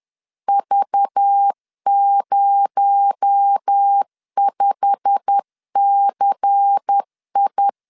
Morse code, heard as a keyed tone, about 10 wpm, 800 Hz. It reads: V05CI
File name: LEZIONI / 8 YJ45 / 8 YJ45 10wpm 2549